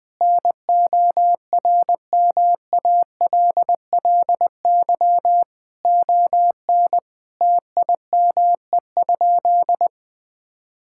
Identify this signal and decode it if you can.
Morse code, heard as a keyed tone, about 20 wpm, 700 Hz. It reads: NORMALLY ON TIME?
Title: Sentences from Top 1000 Words - Review Part 6 - 20wpm